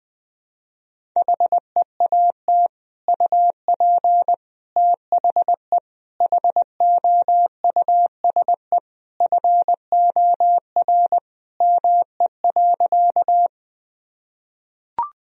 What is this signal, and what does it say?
Morse code, heard as a keyed tone, about 20 wpm, 700 Hz. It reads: HEAT UP THE HOUSE FOR ME.
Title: Sentences from Top 1000 Words - Review Part 3 - 20wpm